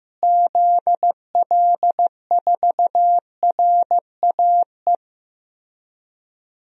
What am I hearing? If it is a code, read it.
Morse code, heard as a keyed tone, about 16 wpm, 700 Hz. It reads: ZL4RAE
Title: DXCC-100 Call Signs - 6 Characters - Repeated 3x - 15wpm